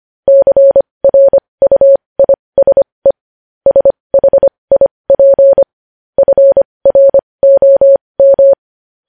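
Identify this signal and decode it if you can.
Morse code, heard as a keyed tone, about 25 wpm, spaced as for 20 wpm, 550 Hz. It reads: CRUISE SHIP FROM
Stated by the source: Morse code news headlines sent at 20.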